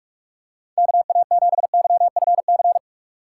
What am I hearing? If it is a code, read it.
Morse code, heard as a keyed tone, about 49 wpm, 700 Hz. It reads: KA7YFC